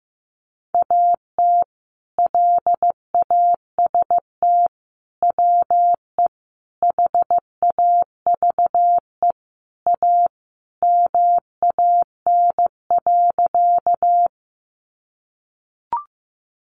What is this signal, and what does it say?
Morse code, heard as a keyed tone, about 15 wpm, 700 Hz. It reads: AT LAST WE HAVE A MAN.
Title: Sentences from Top 300 Words 15wpm